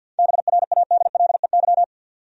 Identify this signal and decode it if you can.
Morse code, heard as a keyed tone, about 52 wpm, 700 Hz. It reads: BRADLEY
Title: USA Common Names - 50wpm